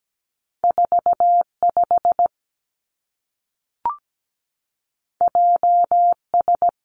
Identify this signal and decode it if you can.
Morse code, heard as a keyed tone, about 17 wpm, 700 Hz. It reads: 4 5 J S
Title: Two Challenging Letter-Number Combinations 17wpm